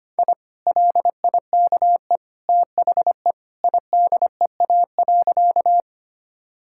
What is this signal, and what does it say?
Morse code, heard as a keyed tone, about 25 wpm, 700 Hz. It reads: I LIKE THE IDEA.
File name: practice / 25wpm / Sentences with 2-Word Phrases - 25wpm